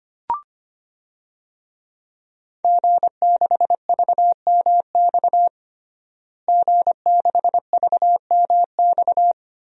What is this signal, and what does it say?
Morse code, heard as a keyed tone, about 25 wpm, 700 Hz. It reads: G6VMX G6VMX
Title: DXCC-100 Call Signs - 5 Characters - Repeated 2x - 25wpm